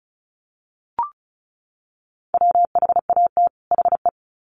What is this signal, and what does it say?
Morse code, heard as a keyed tone, about 34 wpm, 700 Hz. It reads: WHAT HE
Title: Sets of 2 Words 35wpm